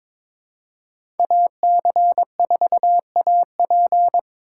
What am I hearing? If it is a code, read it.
Morse code, heard as a keyed tone, about 22 wpm, 700 Hz. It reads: AC4AP
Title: USA Call Signs - 5 Characters - 3x Character Spacing - 22wpm